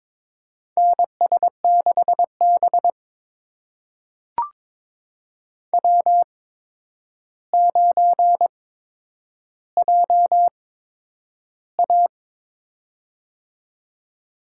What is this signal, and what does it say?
Morse code, heard as a keyed tone, about 22 wpm, spaced as for 9 wpm, 700 Hz. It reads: NS6B W9JA
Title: USA Call Signs - 4 Characters - 8x Character Spacing - 22wpm